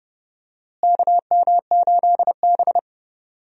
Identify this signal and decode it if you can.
Morse code, heard as a keyed tone, about 30 wpm, 700 Hz. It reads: KM8B
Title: USA Call Signs - 4 Characters - Repeated 2x - 30wpm